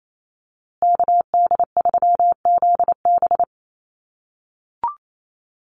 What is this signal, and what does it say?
Morse code, heard as a keyed tone, about 28 wpm, 700 Hz. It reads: KD3ZB
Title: USA Call Signs Repeated 2x 28wpm